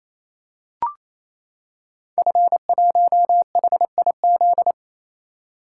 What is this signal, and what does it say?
Morse code, heard as a keyed tone, about 29 wpm, 700 Hz. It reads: F1HIZ
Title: Code Groups - 5 Letter:Numbers - 28wpm